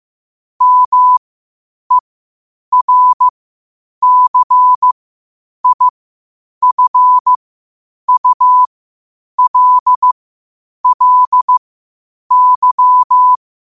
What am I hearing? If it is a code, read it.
Morse code, heard as a keyed tone, about 15 wpm, spaced as for 8 wpm, 1000 Hz. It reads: MERCIFULLY